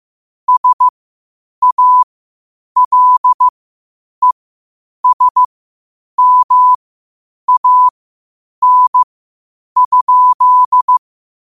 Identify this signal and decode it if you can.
Morse code, heard as a keyed tone, about 15 wpm, spaced as for 8 wpm, 1000 Hz. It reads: SALESMAN?
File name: qotdfiles / 8wpm